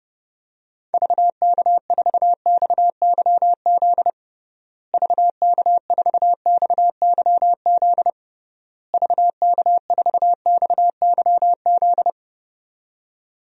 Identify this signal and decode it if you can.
Morse code, heard as a keyed tone, about 30 wpm, 700 Hz. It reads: VK4XYZ VK4XYZ VK4XYZ
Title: DXCC-100 Call Signs - 6 Characters - Repeated 3x - 30wpm